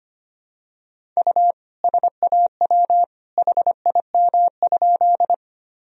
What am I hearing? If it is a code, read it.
Morse code, heard as a keyed tone, about 25 wpm, 700 Hz. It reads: U SAW HIM?
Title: Sentences from Top 300 Words and Abbreviations 25wpm